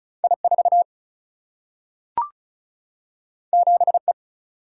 Morse code, heard as a keyed tone, about 36 wpm, 700 Hz. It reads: I4 7E
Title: Single-Letter Single-Number 35wpm